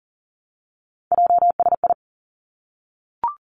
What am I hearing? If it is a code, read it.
Morse code, heard as a keyed tone, about 41 wpm, 700 Hz. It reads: JSI
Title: Three Challenging Letter-Number Combinations 40wpm